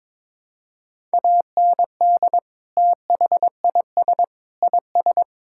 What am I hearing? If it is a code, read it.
Morse code, heard as a keyed tone, about 22 wpm, 700 Hz. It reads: AND THIS IS
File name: practice / 22wpm / Sentences with 3-Word Phrases - 22wpm